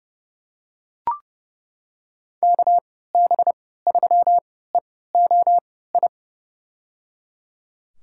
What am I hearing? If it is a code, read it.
Morse code, heard as a keyed tone, about 30 wpm, spaced as for 17 wpm, 700 Hz. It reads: KB3EOI
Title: USA Call Signs - 6 Characters - 3x Character Spacing - 30wpm